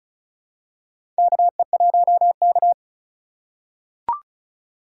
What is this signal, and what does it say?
Morse code, heard as a keyed tone, about 36 wpm, 700 Hz. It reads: KE1K
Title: USA Call Signs - 4 Characters - 4x Character Spacing - 35wpm